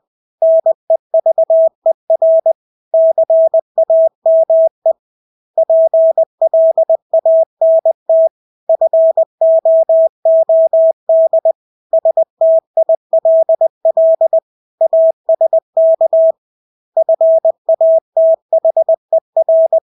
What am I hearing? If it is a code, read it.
Morse code, heard as a keyed tone, about 20 wpm, 650 Hz. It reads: NEVER CAME PLANT FOOD STILL ASK FATHER
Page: Engelske ord 20 wpm